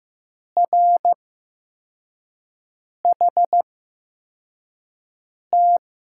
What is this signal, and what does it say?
Morse code, heard as a keyed tone, about 15 wpm, spaced as for 4 wpm, 700 Hz. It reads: RHT
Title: USA Call Signs - 5 Characters - 8x Character Spacing - 15wpm